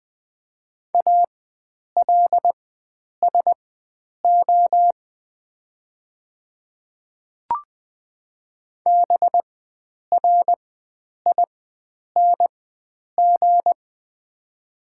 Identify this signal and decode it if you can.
Morse code, heard as a keyed tone, about 20 wpm, spaced as for 9 wpm, 700 Hz. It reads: ALSO BRING
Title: 4x Character Spacing - Top 500 Words - 20wpm